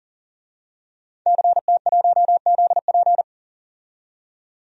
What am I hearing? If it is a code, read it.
Morse code, heard as a keyed tone, about 40 wpm, 700 Hz. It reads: CT1ZP
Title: DXCC-100 Call Signs - 5 Characters - 40wpm